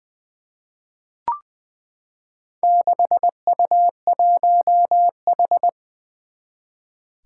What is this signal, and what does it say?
Morse code, heard as a keyed tone, about 20 wpm, 700 Hz. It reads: 6U1H
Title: Four Challenging Letter-Number Combinations 20wpm